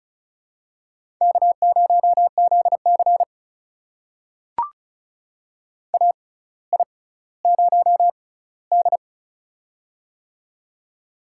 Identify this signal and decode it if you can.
Morse code, heard as a keyed tone, about 36 wpm, spaced as for 18 wpm, 700 Hz. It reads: K0ZC AI0D
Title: USA Call Signs - 4 Characters - 6x Character Spacing - 35wpm